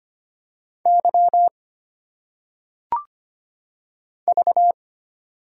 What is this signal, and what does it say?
Morse code, heard as a keyed tone, about 25 wpm, 700 Hz. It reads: Y V
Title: Single Letters - Phonetic Pronunciation 25wpm